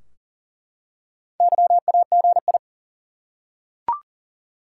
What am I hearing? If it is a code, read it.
Morse code, heard as a keyed tone, about 40 wpm, 700 Hz. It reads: YAGI
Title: Top 180 QSO Elements 40wpm